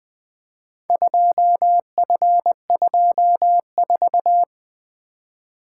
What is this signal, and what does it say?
Morse code, heard as a keyed tone, about 23 wpm, 700 Hz. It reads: 2F24
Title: Speed-Racing - Four Letter-Number Combinations - 20wpm